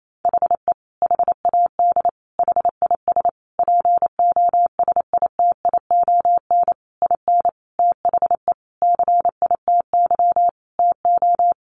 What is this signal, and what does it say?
Morse code, heard as a keyed tone, about 28 wpm, 700 Hz. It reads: HE HAD HIS POSITION IN THE CITY TO
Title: Sentences from Top 900 Words 28wpm